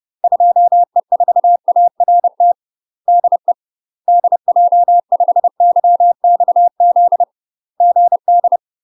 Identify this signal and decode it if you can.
Morse code, heard as a keyed tone, about 30 wpm, 700 Hz. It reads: 2E4ART DE DJ5YXZ GD
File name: UK MASTER QSOs TRAINING / 30WPM / UK MASTER QSOs 0044